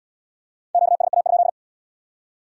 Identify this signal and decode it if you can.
Morse code, heard as a keyed tone, about 64 wpm, 700 Hz. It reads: QSA?